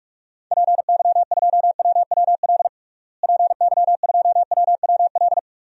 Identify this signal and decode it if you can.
Morse code, heard as a keyed tone, about 46 wpm, 700 Hz. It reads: PY2WWL PY2WWL
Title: DXCC-100 Call Signs - 6 Characters - Repeated 2x - 45wpm